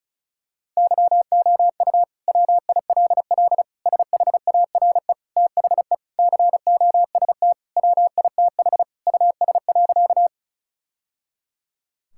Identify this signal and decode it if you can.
Morse code, heard as a keyed tone, about 34 wpm, 700 Hz. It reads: YOU WILL SHARE THE COST WITH US.
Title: Sentences from Top 1000 Words - Review Part 6 - 35wpm